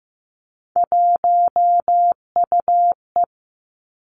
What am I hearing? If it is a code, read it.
Morse code, heard as a keyed tone, about 15 wpm, 700 Hz. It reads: 1UE
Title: Three Challenging Letter-Number Combinations 15wpm